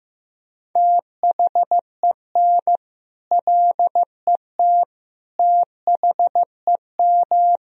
Morse code, heard as a keyed tone, about 15 wpm, 700 Hz. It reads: THEN LET THEM
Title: Sentences from Top 1000 Words - Review Part 3 - 15wpm